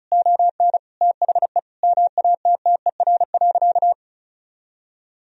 Morse code, heard as a keyed tone, about 36 wpm, 700 Hz. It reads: ON THE MATTER.
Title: Sentences from Top 1000 Words - Review Part 4 - 35wpm